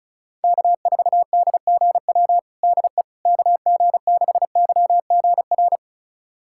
Morse code, heard as a keyed tone, about 34 wpm, 700 Hz. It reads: K4DGW DE KG6YGR
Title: USA Callsign DE Callsign 35wpm